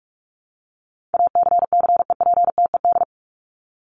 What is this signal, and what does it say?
Morse code, heard as a keyed tone, about 44 wpm, 700 Hz. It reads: ACCEPTED
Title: ED Suffix - Top 100 Words with Base Words 45wpm